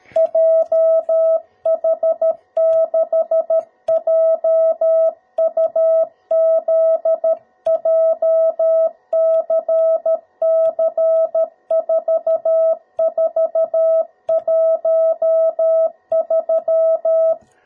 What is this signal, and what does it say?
Morse code, heard as a keyed tone, about 13 wpm, 650 Hz. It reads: JH6JUZ JCC4413
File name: jh8cbh / 2021cw / cw02b